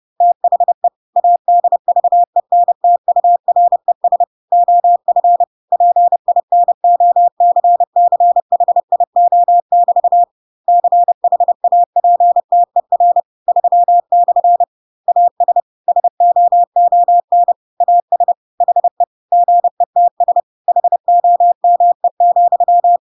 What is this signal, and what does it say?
Morse code, heard as a keyed tone, about 30 wpm, 700 Hz. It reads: THE ADVENTURES OF PINOCCHIO<BT> CHAPTER 3/ AS SOON AS HE GETS HOME,
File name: Pinocchio / Pinocchio ch3 30 30 ik1hge